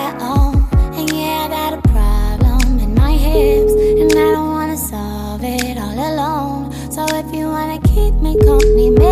Ringback tone (Europe, 425 Hz), with noise.